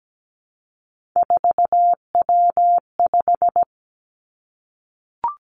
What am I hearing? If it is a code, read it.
Morse code, heard as a keyed tone, about 17 wpm, 700 Hz. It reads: 4W5
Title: Three Challenging Letter-Number Combinations 17wpm